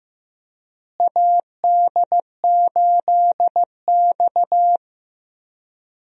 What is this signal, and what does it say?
Morse code, heard as a keyed tone, about 15 wpm, 700 Hz. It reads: AD8X
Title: USA Call Signs - 4 Characters - Repeated 3x - 15wpm